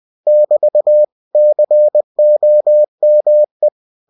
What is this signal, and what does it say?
Morse code, heard as a keyed tone, about 20 wpm, 600 Hz. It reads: <BT> COME
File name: cw-books / winnie-the-pooh / winnie-20wpm-page-0080